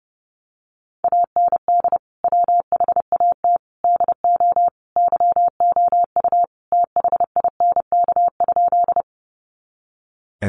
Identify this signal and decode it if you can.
Morse code, heard as a keyed tone, about 30 wpm, 700 Hz. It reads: AND WHAT DO YOU THINK?
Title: Sentences from Top 200 Words 30wpm